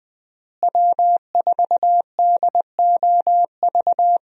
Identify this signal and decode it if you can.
Morse code, heard as a keyed tone, about 20 wpm, 700 Hz. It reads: W4DOV
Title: USA Call Signs - 5 Characters - Repeated 3x - 20wpm